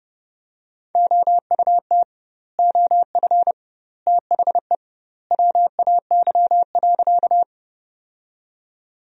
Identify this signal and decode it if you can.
Morse code, heard as a keyed tone, about 30 wpm, 700 Hz. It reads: OUT OF THE WAY.
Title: 2x Word-Spacing - Sentences from Top 100 Words - 30wpm